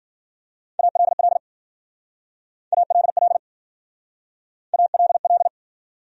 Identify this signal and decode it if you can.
Morse code, heard as a keyed tone, about 51 wpm, 700 Hz. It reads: ALL ALL ALL